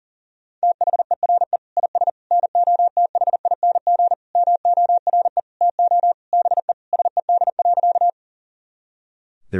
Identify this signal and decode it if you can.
Morse code, heard as a keyed tone, about 37 wpm, 700 Hz. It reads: THERE IS NOTHING MORE TO BE SED.
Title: Sentences from Top 500 Words and Abbreviations 40wpm